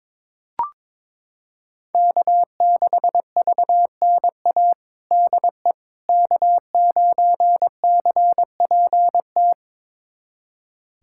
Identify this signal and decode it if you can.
Morse code, heard as a keyed tone, about 22 wpm, 700 Hz. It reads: K6VNA DE K9CPT
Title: USA Callsign DE Callsign - 5 Characters - 22wpm